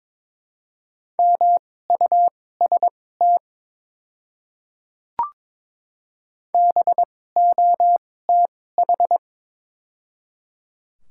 Morse code, heard as a keyed tone, about 22 wpm, spaced as for 16 wpm, 700 Hz. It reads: MUST BOTH
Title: 2x Character Spacing - Top 300 Words - 22wpm